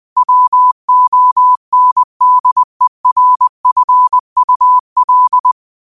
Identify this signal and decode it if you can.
Morse code, heard as a keyed tone, about 20 wpm, 1000 Hz. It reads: WONDERFUL
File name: qotdfiles / 20wpm